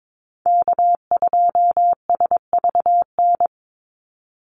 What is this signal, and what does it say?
Morse code, heard as a keyed tone, about 22 wpm, 700 Hz. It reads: K2SVN
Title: USA Call Signs 22wpm